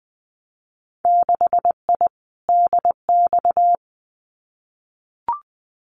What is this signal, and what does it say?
Morse code, heard as a keyed tone, about 20 wpm, 700 Hz. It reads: 6I DX
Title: Winter Field Day Exchanges Encore 20wpm